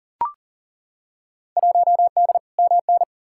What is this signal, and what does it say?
Morse code, heard as a keyed tone, about 40 wpm, 700 Hz. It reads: 1D MN
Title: Field Day Exchanges - 40wpm